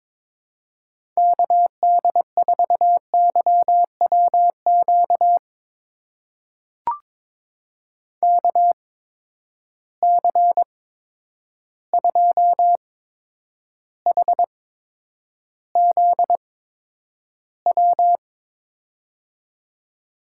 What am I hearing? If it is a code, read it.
Morse code, heard as a keyed tone, about 22 wpm, spaced as for 9 wpm, 700 Hz. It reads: KD4YWQ KC2HZW
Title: USA Call Signs - 6 Characters - 8x Character Spacing - 22wpm